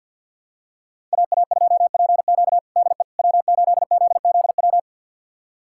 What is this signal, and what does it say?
Morse code, heard as a keyed tone, about 52 wpm, 700 Hz. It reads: AA2PQ DE W8ZZW
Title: USA Callsign DE Callsign 50wpm